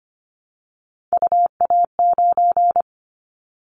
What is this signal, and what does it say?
Morse code, heard as a keyed tone, about 25 wpm, 700 Hz. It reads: UA9
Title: Callsign Prefixes for Top 25 DXCC - Repeated 3x - 25wpm